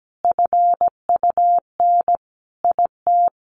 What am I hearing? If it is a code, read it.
Morse code, heard as a keyed tone, about 17 wpm, 700 Hz. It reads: FUN IT
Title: Sentences from Top 700 Words 17wpm